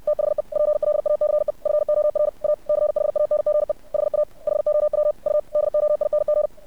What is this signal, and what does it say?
Morse code, heard as a keyed tone, about 60 wpm, 600 Hz. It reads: THE PLACE FOR A CHANGE HA HOW U DOING
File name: archived / CRSnet